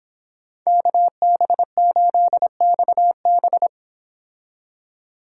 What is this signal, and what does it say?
Morse code, heard as a keyed tone, about 26 wpm, 700 Hz. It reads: KB8XB